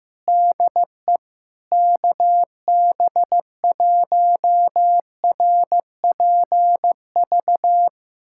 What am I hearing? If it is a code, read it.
Morse code, heard as a keyed tone, about 16 wpm, 700 Hz. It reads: DE KB1RPV
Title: USA Callsign DE Callsign - 6 Characters - 15wpm